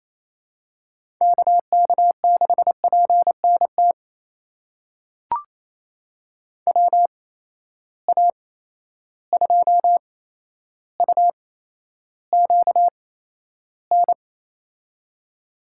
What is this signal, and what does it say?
Morse code, heard as a keyed tone, about 28 wpm, spaced as for 12 wpm, 700 Hz. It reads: KK6PNT WA2UQN
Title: USA Call Signs - 8x Character Spacing - 28wpm